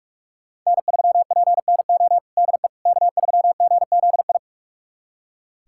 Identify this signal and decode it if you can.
Morse code, heard as a keyed tone, about 41 wpm, 700 Hz. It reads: N3PNO DE K3GZI
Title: USA Callsign DE Callsign - 5 Characters - 45wpm